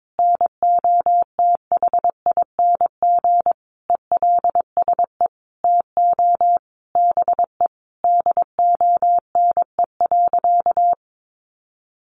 Morse code, heard as a keyed tone, about 22 wpm, 700 Hz. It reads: NOTHING ELSE TO BE DONE.
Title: Sentences from Top 800 Words 22wpm